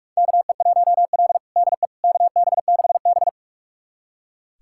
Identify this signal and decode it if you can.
Morse code, heard as a keyed tone, about 46 wpm, 700 Hz. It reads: KE1L DE KB6B
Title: USA Callsign DE Callsign - 4 Characters - 45wpm